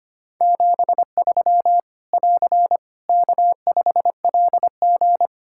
Morse code, heard as a keyed tone, about 25 wpm, 700 Hz. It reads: 73 <AR> K5LG
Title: K1USN SST QSO Practice 25wpm